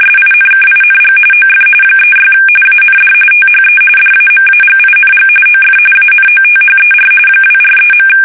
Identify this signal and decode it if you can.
RTTY, about 45.45 baud. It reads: SINKING OF 36 AND THE DAMAGING OF ANOTHER 243.
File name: fsk850